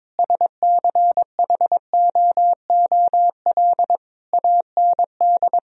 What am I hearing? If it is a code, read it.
Morse code, heard as a keyed tone, about 22 wpm, 700 Hz. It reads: SCHOOL AND
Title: Sentences from Top 1000 Words - Review Part 4 - 22wpm